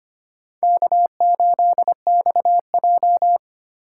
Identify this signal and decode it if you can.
Morse code, heard as a keyed tone, about 25 wpm, 700 Hz. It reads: K8XJ
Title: K1USN SST QSO Practice - 25wpm